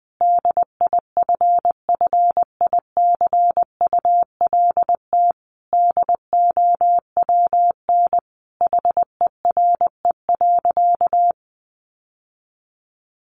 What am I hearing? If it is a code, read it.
Morse code, heard as a keyed tone, about 20 wpm, 700 Hz. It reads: DIFFICULT DOWN HERE.
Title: Sentences from Top 1100 Words 20wpm